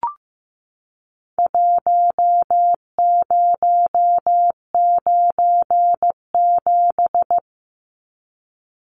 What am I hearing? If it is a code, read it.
Morse code, heard as a keyed tone, about 15 wpm, 700 Hz. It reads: 1097
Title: Numbers - Contest Style - 4-Digit 15wpm